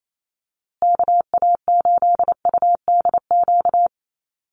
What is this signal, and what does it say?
Morse code, heard as a keyed tone, about 28 wpm, 700 Hz. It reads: KA8UDQ
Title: USA Call Signs Repeated 3x 28wpm